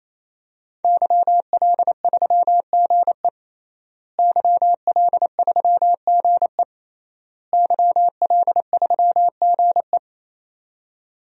Morse code, heard as a keyed tone, about 28 wpm, 700 Hz. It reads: YL3GE YL3GE YL3GE
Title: DXCC-100 Call Signs - 5 Characters - Repeated 3x - 28wpm